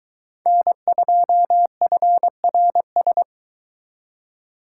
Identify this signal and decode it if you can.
Morse code, heard as a keyed tone, about 22 wpm, 700 Hz. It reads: N2FRS